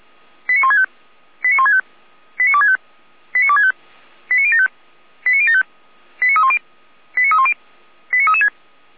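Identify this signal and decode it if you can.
Five-tone selective call (ZVEI), heard as quick runs of five tones, tones of about 80 ms. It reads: 89165 89165 89265 89265 89075 89075 89310 89310 89336
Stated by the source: ZVEI five-tone selective call received on SDR.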